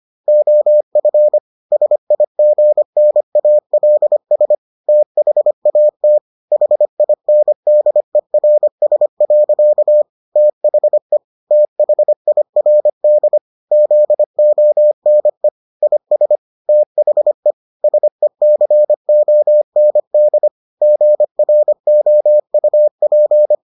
Morse code, heard as a keyed tone, about 25 wpm, 600 Hz. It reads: OF SIGNALS THAT HINDERS. THE THIRD ZONE IS THE SECOND GROUP